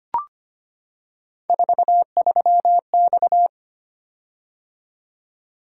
Morse code, heard as a keyed tone, about 25 wpm, 700 Hz. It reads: 43X
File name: Three Letter-Number Combinations - 25wpm